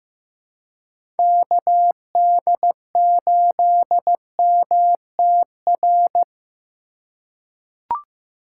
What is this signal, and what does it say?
Morse code, heard as a keyed tone, about 15 wpm, 700 Hz. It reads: KD8MTR
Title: USA Call Signs - 6 Characters - 2x Character Spacing - 15wpm